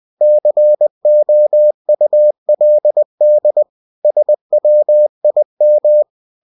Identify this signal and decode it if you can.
Morse code, heard as a keyed tone, about 20 wpm, 600 Hz. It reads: COULD SWIM
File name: cw-books / winnie-the-pooh / winnie-20wpm-page-0104